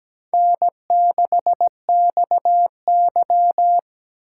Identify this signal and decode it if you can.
Morse code, heard as a keyed tone, about 20 wpm, 700 Hz. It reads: N6XY
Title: Speed Racing - USA Call Signs 17wpm